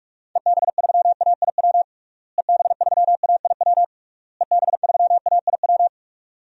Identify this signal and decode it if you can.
Morse code, heard as a keyed tone, about 41 wpm, 700 Hz. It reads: EB3AIW EB3AIW EB3AIW
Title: DXCC-100 Call Signs - 6 Characters - Repeated 3x - 45wpm